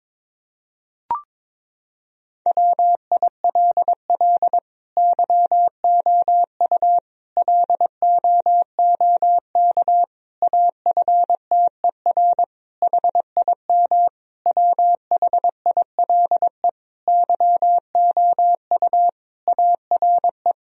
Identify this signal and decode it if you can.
Morse code, heard as a keyed tone, about 22 wpm, 700 Hz. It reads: WILL YOU LOOK AFTER HIM WHILE YOU ARE
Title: Sentences from Top 1000 Words - Review Part 1 - 22wpm